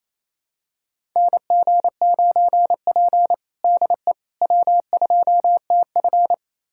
Morse code, heard as a keyed tone, about 28 wpm, 700 Hz. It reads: NG9P DE W2TF
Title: USA Callsign DE Callsign - 4 Characters - 28wpm